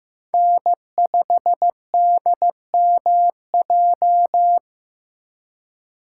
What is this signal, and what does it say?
Morse code, heard as a keyed tone, about 18 wpm, 700 Hz. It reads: N5DMJ
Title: Speed Racing - USA 1x3 Call Signs 15wpm